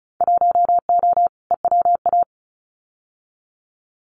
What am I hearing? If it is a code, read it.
Morse code, heard as a keyed tone, about 36 wpm, 700 Hz. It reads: 1O EWA
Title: Winter Field Day Exchanges Encore 35wpm